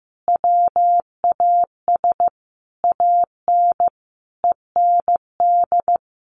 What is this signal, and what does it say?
Morse code, heard as a keyed tone, about 15 wpm, 700 Hz. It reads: WAS AN END
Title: Sentences from Top 1400 Words 15wpm